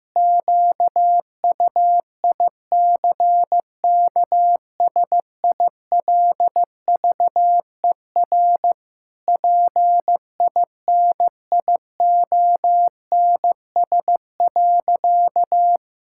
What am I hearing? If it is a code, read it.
Morse code, heard as a keyed tone, about 15 wpm, 700 Hz. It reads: QUICKSILVER PINIONS.
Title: Pangram Challenge v2 - 15wpm